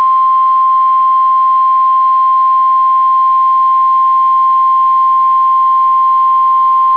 Emergency Alert System (SAME) tones (the 1050 Hz attention tone). It is heard from 1.1 to 6.4 s.